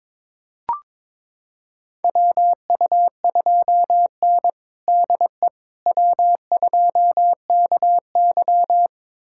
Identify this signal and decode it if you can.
Morse code, heard as a keyed tone, about 22 wpm, 700 Hz. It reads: WU2N DE W2KY
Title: USA Callsign DE Callsign - 4 Characters - 22wpm